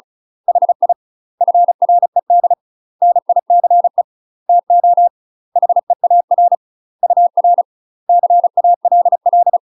Morse code, heard as a keyed tone, about 36 wpm, 700 Hz. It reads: HI FRED NICE TO HEAR UR CALL
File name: UK MASTER QSOs TRAINING / 35WPM / UK MASTER QSOs 0022